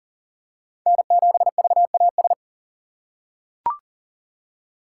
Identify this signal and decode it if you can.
Morse code, heard as a keyed tone, about 40 wpm, 700 Hz. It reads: N7VAS
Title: USA Call Signs - 5 Characters - 4x Character Spacing - 40wpm